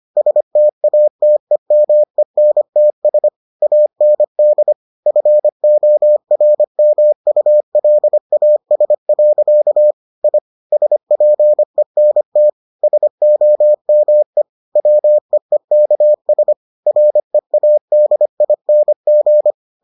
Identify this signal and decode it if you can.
Morse code, heard as a keyed tone, about 25 wpm, 600 Hz. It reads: STATEMENTS AND FORMULAS. I SPENT SOME WEEKS READING